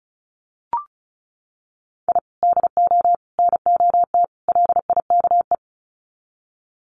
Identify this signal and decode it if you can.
Morse code, heard as a keyed tone, about 34 wpm, 700 Hz. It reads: I DO NOT LIKE
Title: Sets of 4 Words 35wpm